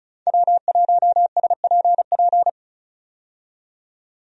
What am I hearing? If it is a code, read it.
Morse code, heard as a keyed tone, about 36 wpm, 700 Hz. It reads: W1SPP
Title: USA Call Signs - 5 Characters - 35wpm